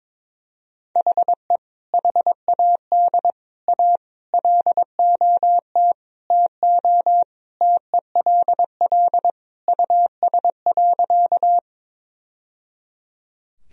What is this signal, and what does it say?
Morse code, heard as a keyed tone, about 22 wpm, 700 Hz. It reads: HE HAD A LOT TO TELL US.
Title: Sentences from Top 1000 Words - Review Part 3 - 22wpm